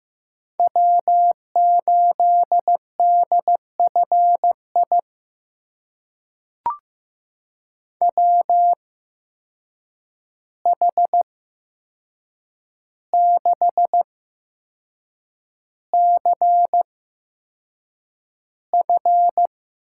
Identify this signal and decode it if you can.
Morse code, heard as a keyed tone, about 15 wpm, spaced as for 6 wpm, 700 Hz. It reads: W8DFI WH6CF
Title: USA Call Signs - 8x Character Spacing - 15wpm